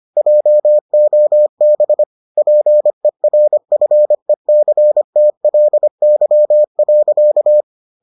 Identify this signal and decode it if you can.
Morse code, heard as a keyed tone, about 25 wpm, 600 Hz. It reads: JOB PERFECTLY.